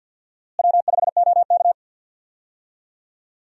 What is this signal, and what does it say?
Morse code, heard as a keyed tone, about 50 wpm, 700 Hz. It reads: W5OK
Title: USA Call Signs - 4 Characters - Repeated 2x - 50wpm